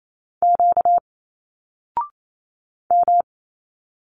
Morse code, heard as a keyed tone, about 28 wpm, 700 Hz. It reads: Q M
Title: Single Letters 28wpm